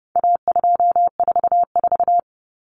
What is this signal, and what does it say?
Morse code, heard as a keyed tone, about 30 wpm, 700 Hz. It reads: A244
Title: Common Cut Numbers - 4-Digit 30wpm